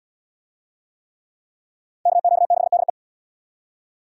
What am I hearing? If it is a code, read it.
Morse code, heard as a keyed tone, about 67 wpm, 700 Hz. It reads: KQ6DE